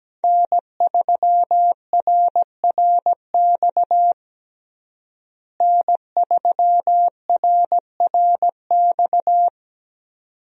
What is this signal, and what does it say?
Morse code, heard as a keyed tone, about 17 wpm, 700 Hz. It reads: N3RRX N3RRX
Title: USA Call Signs - 5 Characters - Repeated 2x - 17wpm